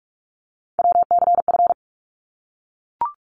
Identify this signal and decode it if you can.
Morse code, heard as a keyed tone, about 44 wpm, 700 Hz. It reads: WCF
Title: ARRL Sections 45wpm